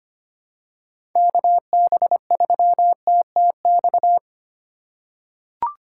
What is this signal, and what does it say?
Morse code, heard as a keyed tone, about 25 wpm, 700 Hz. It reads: KB3TTX
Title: USA Call Signs - 6 Characters - 6x Character Spacing - 25wpm